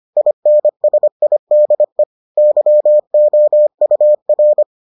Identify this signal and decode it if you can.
Morse code, heard as a keyed tone, about 25 wpm, 600 Hz. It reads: INSIDE YOUR